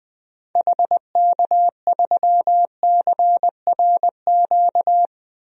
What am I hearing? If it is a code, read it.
Morse code, heard as a keyed tone, about 20 wpm, 700 Hz. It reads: HK3CRQ
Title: DXCC-100 Call Signs - 6 Characters - 20wpm